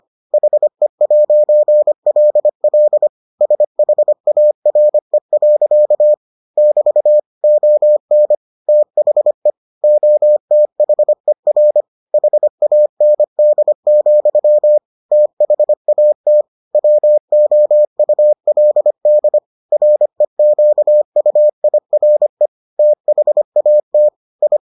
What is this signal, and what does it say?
Morse code, heard as a keyed tone, about 25 wpm, 600 Hz. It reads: HE'LL SHARE. <BT> ON THE OTHER HAND, THAT WOULD REQUIRE THAT I